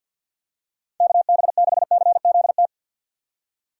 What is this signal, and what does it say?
Morse code, heard as a keyed tone, about 52 wpm, 700 Hz. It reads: KB6CZT
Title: USA Call Signs - 6 Characters - 6x Character Spacing - 50wpm